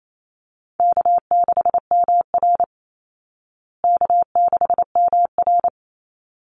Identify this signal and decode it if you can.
Morse code, heard as a keyed tone, about 28 wpm, 700 Hz. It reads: K6MR K6MR
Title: Easy Call Signs Repeated 2x - 28wpm